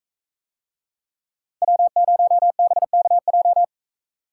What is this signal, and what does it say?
Morse code, heard as a keyed tone, about 41 wpm, 700 Hz. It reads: W0BKJ